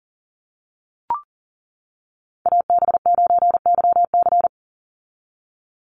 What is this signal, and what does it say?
Morse code, heard as a keyed tone, about 38 wpm, 700 Hz. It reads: AB9YC
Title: Call Signs - 40wpm